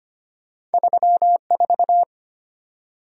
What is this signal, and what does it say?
Morse code, heard as a keyed tone, about 32 wpm, 700 Hz. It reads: 34